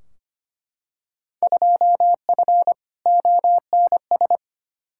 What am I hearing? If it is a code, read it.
Morse code, heard as a keyed tone, about 25 wpm, 700 Hz. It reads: 2F ONS
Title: Field Day Exchanges - 25wpm